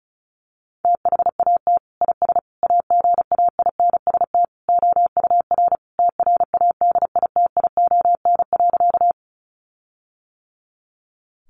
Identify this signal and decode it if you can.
Morse code, heard as a keyed tone, about 34 wpm, 700 Hz. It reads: THAT IS AGAINST OUR TRADITION.
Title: Sentences from Top 1500 Words 35wpm